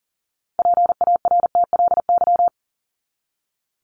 Morse code, heard as a keyed tone, about 40 wpm, 700 Hz. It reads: PARTLY
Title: LY Suffix - Top 100 Words with Base Words 40wpm